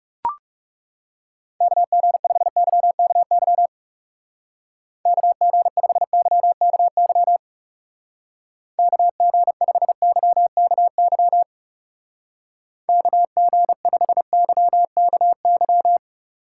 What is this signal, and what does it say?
Morse code, heard as a keyed tone, about 40 wpm, 700 Hz. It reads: KG5YKY KG5YKY KG5YKY KG5YKY